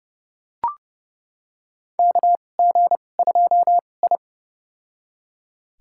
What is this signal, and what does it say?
Morse code, heard as a keyed tone, about 30 wpm, spaced as for 22 wpm, 700 Hz. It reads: KG2I
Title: USA Call Signs - 4 Characters - 2x Character Spacing - 30wpm